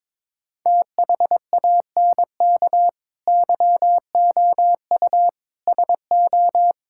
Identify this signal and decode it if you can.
Morse code, heard as a keyed tone, about 22 wpm, 700 Hz. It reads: THANK YOU SO
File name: practice / 22wpm / Sentences with 2-Word Phrases - 22wpm